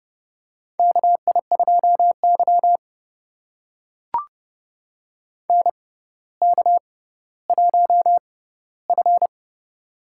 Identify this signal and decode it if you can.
Morse code, heard as a keyed tone, about 30 wpm, spaced as for 15 wpm, 700 Hz. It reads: KI2Y NK1F
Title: USA Call Signs - 4 Characters - 6x Character Spacing - 30wpm